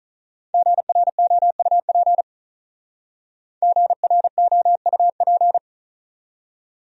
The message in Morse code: GROUP GROUP

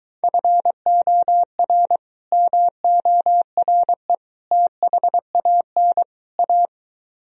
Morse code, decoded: FOR MORE THAN A

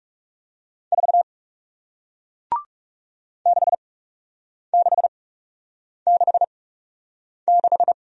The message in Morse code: 4 6666

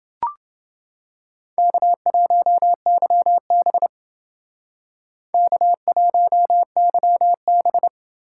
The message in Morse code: K1YB K1YB

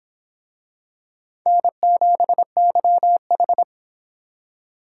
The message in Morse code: N7YH

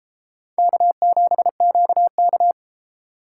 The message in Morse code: K7QK

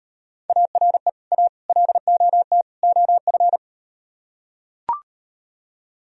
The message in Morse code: ARE A LOT OF

